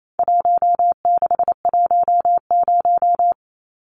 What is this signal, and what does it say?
Morse code, heard as a keyed tone, about 28 wpm, 700 Hz. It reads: 1610
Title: Numbers - 4-Digit 28wpm